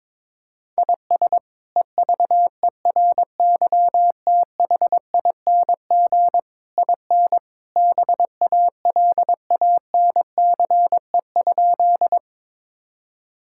Morse code, heard as a keyed tone, about 22 wpm, 700 Hz. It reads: IS EVERYTHING IN BALANCE?
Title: Sentences from Top 1000 Words - Review Part 6 - 22wpm